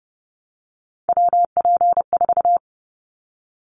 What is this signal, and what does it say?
Morse code, heard as a keyed tone, about 29 wpm, 700 Hz. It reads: WP4
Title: Callsign Prefixes for Top 25 DXCC - 30wpm